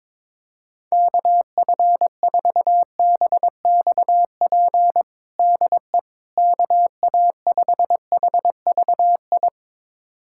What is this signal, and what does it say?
Morse code, heard as a keyed tone, about 22 wpm, 700 Hz. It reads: KF4BXP DE KA5HVI
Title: USA Callsign DE Callsign - 6 Characters - 22wpm